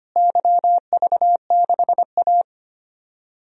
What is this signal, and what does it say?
Morse code, heard as a keyed tone, about 25 wpm, 700 Hz. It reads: YV6A
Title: DXCC-100 Call Signs - 4 Characters - Repeated 2x - 25wpm